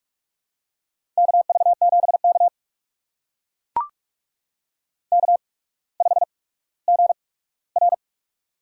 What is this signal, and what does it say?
Morse code, heard as a keyed tone, about 44 wpm, spaced as for 19 wpm, 700 Hz. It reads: KV7K K5GR
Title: USA Call Signs - 4 Characters - 8x Character Spacing - 45wpm